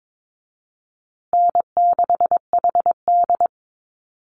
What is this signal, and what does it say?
Morse code, heard as a keyed tone, about 22 wpm, 700 Hz. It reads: N6HD
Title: Easy Call Signs Repeated 2x - 22wpm